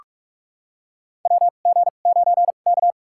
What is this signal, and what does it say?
Morse code, heard as a keyed tone, about 44 wpm, spaced as for 32 wpm, 700 Hz. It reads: WG9K
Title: USA Call Signs - 4 Characters - 2x Character Spacing - 45wpm